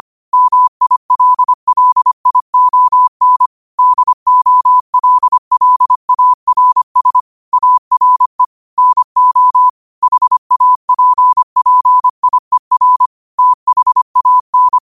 Morse code, heard as a keyed tone, about 25 wpm, 1000 Hz. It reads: MILLION DOLLARS ARE NO HAPPIER THAN
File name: qotdfiles / 25wpm